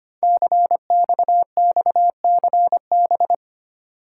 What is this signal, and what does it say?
Morse code, heard as a keyed tone, about 25 wpm, 700 Hz. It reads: CXXCB
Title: Code Groups - 5 Letters - 25wpm